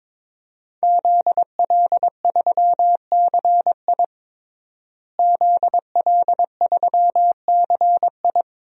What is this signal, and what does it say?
Morse code, heard as a keyed tone, about 22 wpm, 700 Hz. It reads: ZL3CI ZL3CI
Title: DXCC-100 Call Signs - 5 Characters - Repeated 2x - 22wpm